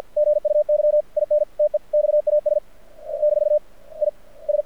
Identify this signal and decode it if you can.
Morse code, heard as a keyed tone, about 52 wpm, spaced as for 39 wpm, 600 Hz. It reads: GUY IN TEXAS - IS
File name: archived / CRSnet